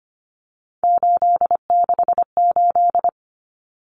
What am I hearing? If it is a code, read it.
Morse code, heard as a keyed tone, about 25 wpm, 700 Hz. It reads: 868
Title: Numbers - 3-Digit 25wpm